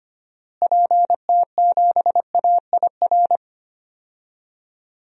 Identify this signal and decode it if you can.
Morse code, heard as a keyed tone, about 25 wpm, 700 Hz. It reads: PT7AIR
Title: DXCC-100 Call Signs - 6 Characters - Repeated 2x - 25wpm